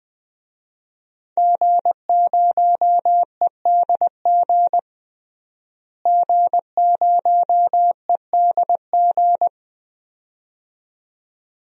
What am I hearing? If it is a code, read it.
Morse code, heard as a keyed tone, about 20 wpm, 700 Hz. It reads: G0EDG G0EDG
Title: DXCC-100 Call Signs - 5 Characters - Repeated 2x - 20wpm